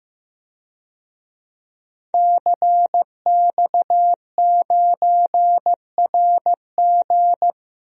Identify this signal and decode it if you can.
Morse code, heard as a keyed tone, about 15 wpm, 700 Hz. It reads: CX9RG
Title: DXCC-100 Call Signs - 5 Characters - Repeated 2x - 15wpm